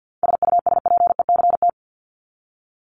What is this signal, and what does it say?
Morse code, heard as a keyed tone, about 50 wpm, 700 Hz. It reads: SUSPECT
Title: Top 1751-2000 Words 50wpm